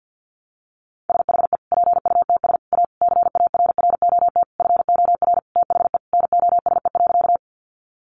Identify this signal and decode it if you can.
Morse code, heard as a keyed tone, about 50 wpm, 700 Hz. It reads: SHE PUTS A CARROT FOR THE NOSE.
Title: Easy Phrases 50wpm